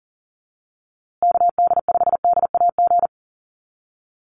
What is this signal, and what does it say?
Morse code, heard as a keyed tone, about 40 wpm, 700 Hz. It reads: KD5DAG
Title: USA Call Signs Repeated 2x 40wpm